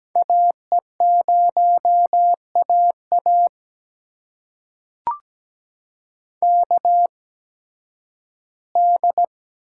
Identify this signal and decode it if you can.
Morse code, heard as a keyed tone, about 17 wpm, spaced as for 11 wpm, 700 Hz. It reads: AE0AA KD
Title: USA Call Signs - 8x Character Spacing - 17wpm